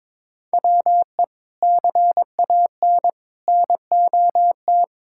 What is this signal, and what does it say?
Morse code, heard as a keyed tone, about 22 wpm, 700 Hz. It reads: WE CAN NOT
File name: practice / 22wpm / Sentences with 3-Word Phrases - 22wpm